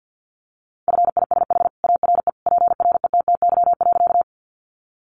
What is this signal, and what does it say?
Morse code, heard as a keyed tone, about 48 wpm, 700 Hz. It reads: FISH ARE PRETTY.
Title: Easy Phrases 50wpm